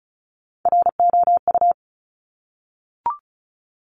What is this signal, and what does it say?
Morse code, heard as a keyed tone, about 36 wpm, 700 Hz. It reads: ROU
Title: Top Three Letter Combinations in Words 35wpm